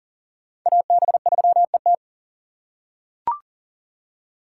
Morse code, heard as a keyed tone, about 40 wpm, 700 Hz. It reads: AB3ET
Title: USA Call Signs - 3x Character Spacing - 40wpm